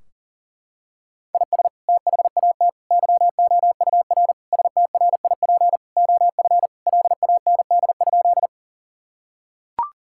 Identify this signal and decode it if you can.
Morse code, heard as a keyed tone, about 40 wpm, 700 Hz. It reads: IS THAT YOUR STRIP OF LAND?
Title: Sentences from Top 1600 Words 40wpm